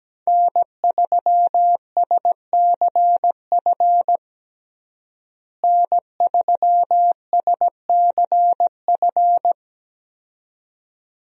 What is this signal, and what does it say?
Morse code, heard as a keyed tone, about 17 wpm, 700 Hz. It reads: N3SCF N3SCF
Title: USA Call Signs - 5 Characters - Repeated 2x - 17wpm